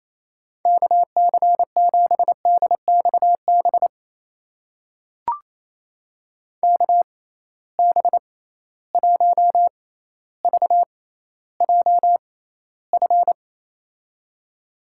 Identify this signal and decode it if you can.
Morse code, heard as a keyed tone, about 28 wpm, spaced as for 14 wpm, 700 Hz. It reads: KC7DXB KB1VJF
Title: USA Call Signs - 6 Characters - 6x Character Spacing - 28wpm